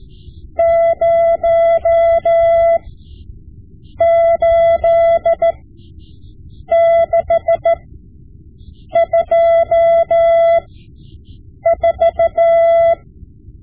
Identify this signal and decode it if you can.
Morse code, heard as a keyed tone, about 14 wpm, spaced as for 6 wpm, 670 Hz. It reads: 08624
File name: Morse Lessons / Morse Lessons / Tape-5B